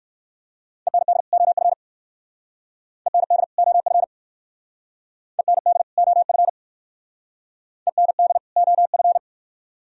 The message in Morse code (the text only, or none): END OF END OF END OF END OF